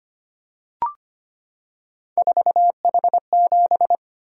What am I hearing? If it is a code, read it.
Morse code, heard as a keyed tone, about 25 wpm, 700 Hz. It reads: 4H7
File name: Three Letter-Number Combinations - 25wpm